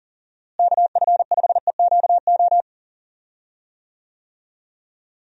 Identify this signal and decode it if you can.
Morse code, heard as a keyed tone, about 37 wpm, 700 Hz. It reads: KF5EQO
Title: USA Call Signs - 6 Characters - Repeated 2x - 40wpm